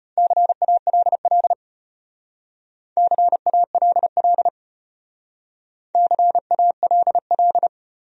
Morse code, heard as a keyed tone, about 33 wpm, 700 Hz. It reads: CALL CALL CALL